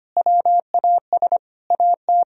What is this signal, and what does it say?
Morse code, heard as a keyed tone, about 25 wpm, 700 Hz. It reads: WAS AT
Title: Sentences from Top 1000 Words - Review Part 2 - 25wpm